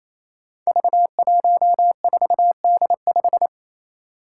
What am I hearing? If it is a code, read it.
Morse code, heard as a keyed tone, about 28 wpm, 700 Hz. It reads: V14D5
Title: Five Challenging Letter-Number Combinations 28wpm